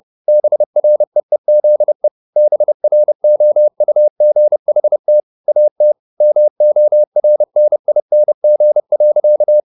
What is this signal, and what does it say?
Morse code, heard as a keyed tone, about 30 wpm, 600 Hz. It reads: BREEZE BROUGHT AT MORNING.
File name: the old man and the sea / the old man and the sea 1 30wpm